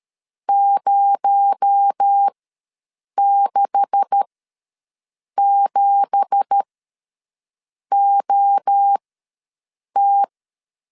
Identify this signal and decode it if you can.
Morse code, heard as a keyed tone, about 13 wpm, spaced as for 6 wpm, 800 Hz. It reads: 067OT
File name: LEZIONI / 9 2867 ws 12wpm 3431